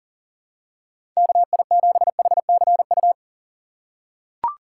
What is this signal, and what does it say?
Morse code, heard as a keyed tone, about 37 wpm, 700 Hz. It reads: KI7HCU
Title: USA Call Signs - 6 Characters - 2x Character Spacing - 40wpm